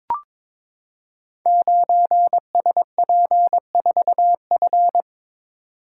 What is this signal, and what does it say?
Morse code, heard as a keyed tone, about 22 wpm, 700 Hz. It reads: 9SP4F
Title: Code Groups - 5 Letter:Numbers - 22wpm